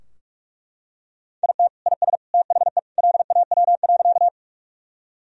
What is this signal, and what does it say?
Morse code, heard as a keyed tone, about 41 wpm, 700 Hz. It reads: IT IS THE LAW.
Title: Sentences from Top 1000 Words - Review Part 3 - 45wpm